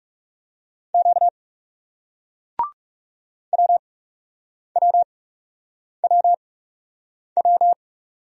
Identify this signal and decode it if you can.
Morse code, heard as a keyed tone, about 40 wpm, spaced as for 8 wpm, 700 Hz. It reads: Q WWWW